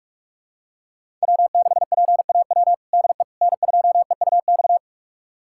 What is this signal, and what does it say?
Morse code, heard as a keyed tone, about 46 wpm, 700 Hz. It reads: W6PAW DE N2EUX
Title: USA Callsign DE Callsign - 5 Characters - 45wpm